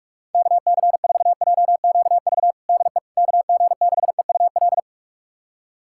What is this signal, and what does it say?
Morse code, heard as a keyed tone, about 46 wpm, 700 Hz. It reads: KC4JQV DE KG6EUL
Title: USA Callsign DE Callsign - 6 Characters - 45wpm